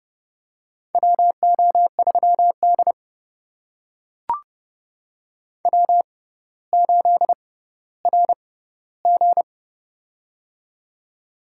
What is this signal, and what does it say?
Morse code, heard as a keyed tone, about 29 wpm, spaced as for 15 wpm, 700 Hz. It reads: WO3D W8RG
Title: USA Call Signs - 4 Characters - 6x Character Spacing - 30wpm